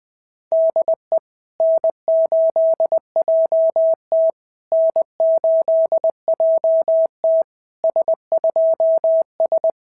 Morse code, heard as a keyed tone, about 21 wpm, 690 Hz. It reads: DE N8JT N8JT S2S
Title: SOTA QSO Practice 20wpm